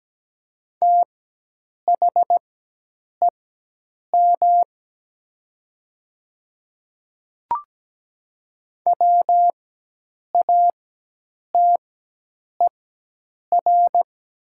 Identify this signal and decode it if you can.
Morse code, heard as a keyed tone, about 17 wpm, spaced as for 8 wpm, 700 Hz. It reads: THEM WATER
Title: 4x Character Spacing - Top 100 Words - 17wpm